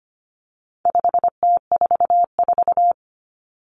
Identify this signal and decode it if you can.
Morse code, heard as a keyed tone, about 25 wpm, 700 Hz. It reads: 5T44
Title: Common Cut Numbers - 4-Digit 25wpm